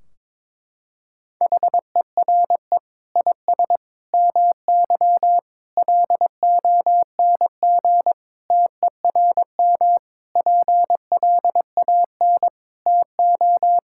Morse code, heard as a keyed tone, about 23 wpm, 700 Hz. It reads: HERE IS MY LONG TERM PLAN TO
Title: Sentences from Top 1600 Words 22wpm